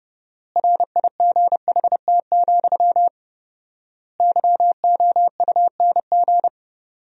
Morse code, heard as a keyed tone, about 32 wpm, 700 Hz. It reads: RIGHT, YOUNG
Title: 4x Word-Spacing - Sentences from Top 400 Words - 30wpm